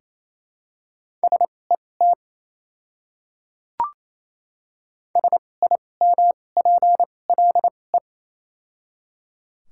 Morse code, heard as a keyed tone, about 28 wpm, spaced as for 20 wpm, 700 Hz. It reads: SET SIMPLE
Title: 2x Character Spacing - Top 500 Words - 28wpm